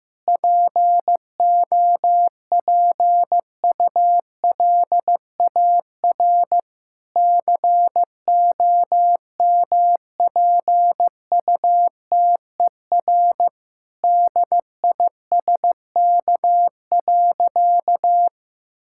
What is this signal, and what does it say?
Morse code, heard as a keyed tone, about 15 wpm, 700 Hz. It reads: POPULAR COMPUTER DISK.
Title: Sentences from Top 1000 Words - Review Part 6 - 15wpm